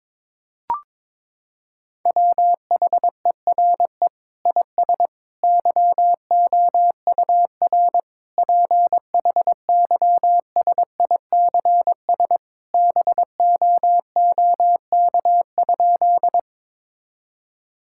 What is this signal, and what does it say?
Morse code, heard as a keyed tone, about 22 wpm, 700 Hz. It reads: WHERE IS YOUR PHYSICS BOOK?
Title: Sentences from Top 1000 Words - Review Part 5 - 22wpm